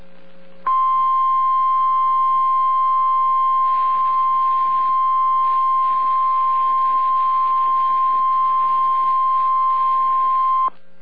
Emergency Alert System (SAME) tones (the 1050 Hz attention tone). It is heard from 0.7 to 10.7 s.